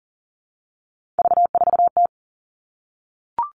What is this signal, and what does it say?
Morse code, heard as a keyed tone, about 40 wpm, 700 Hz. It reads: V4T
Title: All Cut Numbers - 3-Digit 40wpm